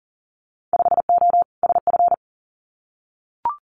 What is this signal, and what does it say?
Morse code, heard as a keyed tone, about 37 wpm, 700 Hz. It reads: 5O SF
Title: Winter Field Day Exchanges Encore 40wpm